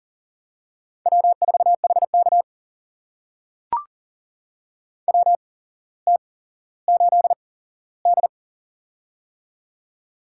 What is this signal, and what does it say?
Morse code, heard as a keyed tone, about 40 wpm, spaced as for 17 wpm, 700 Hz. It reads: W4HK WT8D
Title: USA Call Signs - 4 Characters - 8x Character Spacing - 40wpm